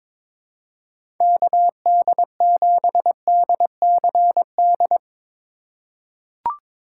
Morse code, heard as a keyed tone, about 22 wpm, 700 Hz. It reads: KD7DCD
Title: USA Call Signs - 4x Character Spacing - 22wpm